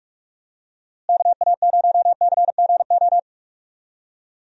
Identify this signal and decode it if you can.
Morse code, heard as a keyed tone, about 41 wpm, 700 Hz. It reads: KA0CGO
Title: USA Call Signs - 6 Characters - Repeated 3x - 45wpm